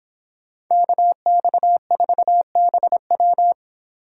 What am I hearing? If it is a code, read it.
Morse code, heard as a keyed tone, about 24 wpm, 700 Hz. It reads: KX4BW